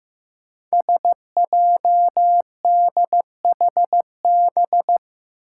Morse code, heard as a keyed tone, about 15 wpm, 700 Hz. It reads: SJDHB
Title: Five Challenging Letter-Number Combinations 15wpm